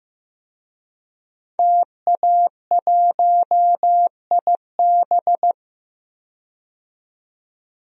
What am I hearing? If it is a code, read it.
Morse code, heard as a keyed tone, about 15 wpm, 700 Hz. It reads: TA1IB
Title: DXCC-100 Call Signs - 5 Characters - 15wpm